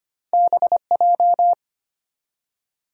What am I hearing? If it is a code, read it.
Morse code, heard as a keyed tone, about 25 wpm, 700 Hz. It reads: BJ
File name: Two Letter-Number Combinations - 25wpm